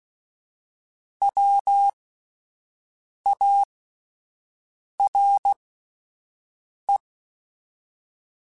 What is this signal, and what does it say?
Morse code, heard as a keyed tone, about 16 wpm, spaced as for 5 wpm, 800 Hz. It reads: WARE
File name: General / General-mp3s / Houses @10